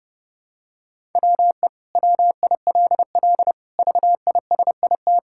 Morse code, heard as a keyed tone, about 30 wpm, 700 Hz. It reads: WE WILL VISIT
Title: Sentences from Top 1000 Words - Review Part 6 - 30wpm